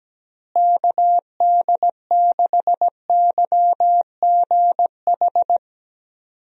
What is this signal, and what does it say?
Morse code, heard as a keyed tone, about 17 wpm, 700 Hz. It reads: KD6YGH
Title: USA Call Signs - 6 Characters - Repeated 2x - 17wpm